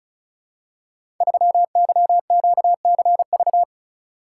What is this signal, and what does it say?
Morse code, heard as a keyed tone, about 36 wpm, 700 Hz. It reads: 3YQCV
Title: Code Groups - 5 Letter:Numbers - 35wpm